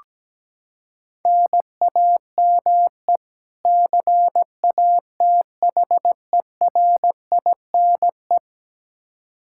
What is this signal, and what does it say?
Morse code, heard as a keyed tone, about 17 wpm, 700 Hz. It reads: NAME CATHERINE
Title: Practice QSO - Exchange 1 - Beginning - 17wpm